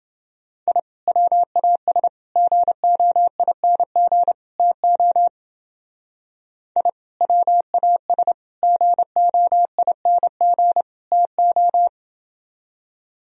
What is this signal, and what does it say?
Morse code, heard as a keyed tone, about 30 wpm, 700 Hz. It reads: I WAS GOING TO I WAS GOING TO